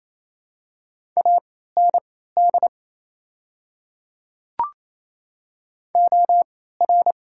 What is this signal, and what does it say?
Morse code, heard as a keyed tone, about 28 wpm, spaced as for 16 wpm, 700 Hz. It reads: AND OR
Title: 3x Character Spacing - Top 100 Words - 28wpm